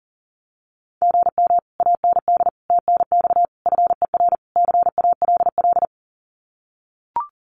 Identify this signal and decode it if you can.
Morse code, heard as a keyed tone, about 40 wpm, 700 Hz. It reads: GM AND TNX FER CALL
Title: Common QSO Phrases Encore 40wpm